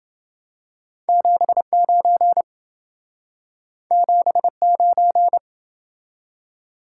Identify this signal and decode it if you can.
Morse code, heard as a keyed tone, about 28 wpm, 700 Hz. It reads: 79 79